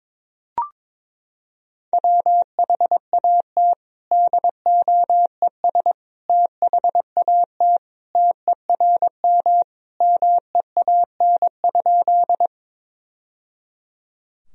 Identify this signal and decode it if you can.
Morse code, heard as a keyed tone, about 22 wpm, 700 Hz. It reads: WHAT DOES THAT TERM MEAN?
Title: Sentences from Top 1000 Words - Review Part 4 - 22wpm